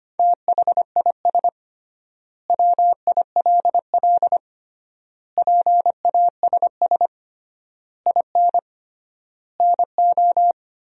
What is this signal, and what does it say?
Morse code, heard as a keyed tone, about 25 wpm, 700 Hz. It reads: THIS WILL PASS IN NO
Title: 3x Word-Spacing - Sentences from Top 400 Words - 25wpm